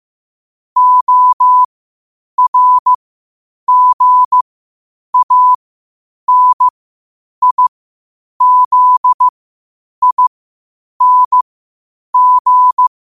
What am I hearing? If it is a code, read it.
Morse code, heard as a keyed tone, about 15 wpm, spaced as for 8 wpm, 1000 Hz. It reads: ORGANIZING
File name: qotdfiles / 8wpm